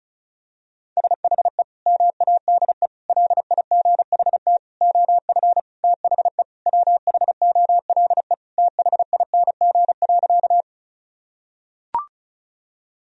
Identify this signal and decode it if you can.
Morse code, heard as a keyed tone, about 36 wpm, 700 Hz. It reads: SHE MADE LIGHT OF THE WHOLE THING.
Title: Sentences from Top 1000 Words - Review Part 1 - 35wpm